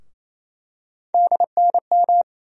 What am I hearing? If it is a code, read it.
Morse code, heard as a keyed tone, about 28 wpm, 700 Hz. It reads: DNM
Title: USA Call Sign Postfixes - 3 Letters 28wpm